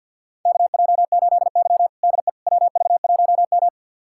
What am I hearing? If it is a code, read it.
Morse code, heard as a keyed tone, about 46 wpm, 700 Hz. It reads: KJ8Y DE WV1M
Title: USA Callsign DE Callsign - 4 Characters - 50wpm